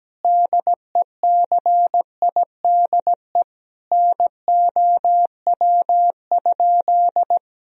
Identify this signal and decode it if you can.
Morse code, heard as a keyed tone, about 17 wpm, 700 Hz. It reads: DECIDE NOW?
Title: Sentences from Top 1000 Words - Review Part 3 - 17wpm